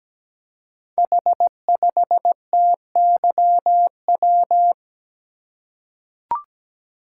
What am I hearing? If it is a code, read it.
Morse code, heard as a keyed tone, about 17 wpm, 700 Hz. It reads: H5TYW
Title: Code Groups - 5 Letter:Numbers - 17wpm